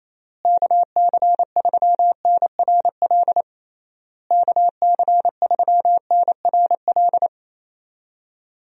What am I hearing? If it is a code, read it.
Morse code, heard as a keyed tone, about 28 wpm, 700 Hz. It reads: KC3NRL KC3NRL
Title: USA Call Signs - 6 Characters - Repeated 3x - 28wpm